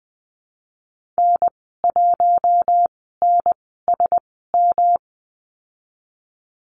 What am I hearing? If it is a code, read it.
Morse code, heard as a keyed tone, about 20 wpm, spaced as for 14 wpm, 700 Hz. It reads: N1NSM
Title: 147 Lesson 24g - Letter y - Call Signs 20wpm